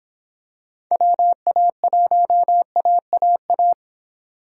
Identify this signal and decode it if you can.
Morse code, heard as a keyed tone, about 26 wpm, 700 Hz. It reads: WA1AAA